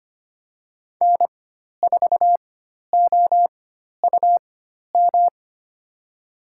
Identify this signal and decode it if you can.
Morse code, heard as a keyed tone, about 25 wpm, spaced as for 12 wpm, 700 Hz. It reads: N4OUM
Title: USA Call Signs - 5 Characters - 4x Character Spacing - 25wpm